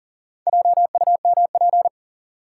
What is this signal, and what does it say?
Morse code, heard as a keyed tone, about 40 wpm, 700 Hz. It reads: JUMP
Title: Sentences from Top 1300 Words - New Word Drill Alt - 40wpm